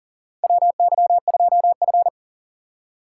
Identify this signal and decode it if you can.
Morse code, heard as a keyed tone, about 40 wpm, 700 Hz. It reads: WY2F
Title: USA Call Signs - 4 Characters - 3x Character Spacing - 40wpm